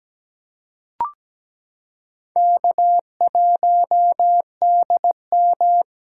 Morse code, heard as a keyed tone, about 17 wpm, 700 Hz. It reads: K1DM
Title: K1USN SST QSO Practice - 17wpm